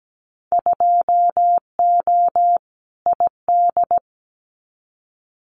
Morse code, heard as a keyed tone, about 17 wpm, 700 Hz. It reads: 2O ID
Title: Winter Field Day Exchanges 17wpm